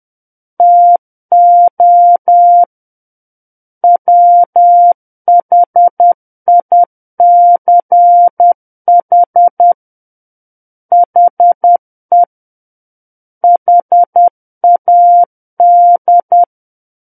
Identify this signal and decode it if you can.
Morse code, heard as a keyed tone, about 10 wpm, 700 Hz. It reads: TO WHICH HE HAD
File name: Book One / 01-Chapter 1 - 10 WPM